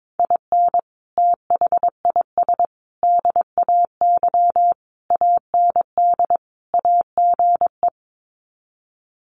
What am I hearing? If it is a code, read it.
Morse code, heard as a keyed tone, about 22 wpm, 700 Hz. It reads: IN THIS DAY AND AGE
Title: Sets of 5 Words Encore 22wpm